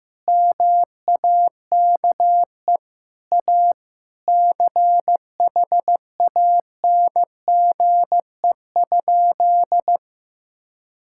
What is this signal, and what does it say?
Morse code, heard as a keyed tone, about 16 wpm, 700 Hz. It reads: MAKE A CHANGE?
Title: Sentences from Top 1600 Words 15wpm